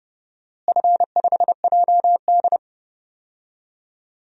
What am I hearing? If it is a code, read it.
Morse code, heard as a keyed tone, about 30 wpm, 700 Hz. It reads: F5JD
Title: DXCC-100 Call Signs - 4 Characters - Repeated 3x - 30wpm